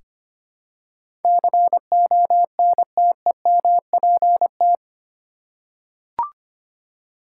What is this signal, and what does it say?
Morse code, heard as a keyed tone, about 25 wpm, 700 Hz. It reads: CONTEMPT
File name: 8 Letter Words - First 3 Letters Given - 25wpm